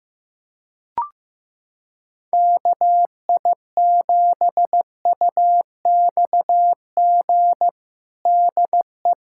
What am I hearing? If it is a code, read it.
Morse code, heard as a keyed tone, about 15 wpm, 700 Hz. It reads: KI7UXG DE
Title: USA Callsign DE Callsign - 6 Characters - 15wpm